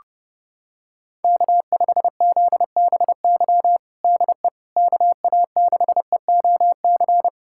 Morse code, heard as a keyed tone, about 30 wpm, 700 Hz. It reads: K5ZBY DE KA6EOC
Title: USA Callsign DE Callsign 30wpm